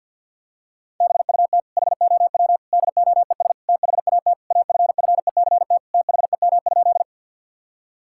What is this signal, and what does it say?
Morse code, heard as a keyed tone, about 50 wpm, 700 Hz. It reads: BUT HOW DOES THAT AFFECT THEM?
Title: Sentences from Top 1000 Words - Review Part 6 - 50wpm